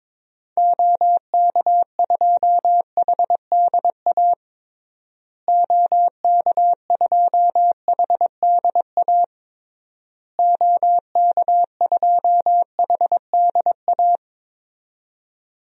Morse code, heard as a keyed tone, about 21 wpm, 700 Hz. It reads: OK2HDA OK2HDA OK2HDA
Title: DXCC-100 Call Signs - 6 Characters - Repeated 3x - 22wpm